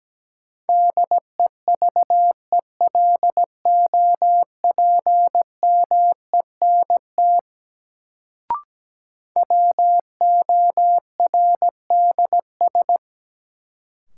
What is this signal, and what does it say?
Morse code, heard as a keyed tone, about 17 wpm, 700 Hz. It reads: DEVELOPMENT WORDS
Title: Sets of 3 Words - New Word Drill - 17wpm